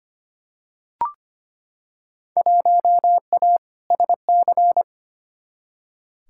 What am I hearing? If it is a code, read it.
Morse code, heard as a keyed tone, about 25 wpm, 700 Hz. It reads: 1A SC
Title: Field Day Exchanges - 25wpm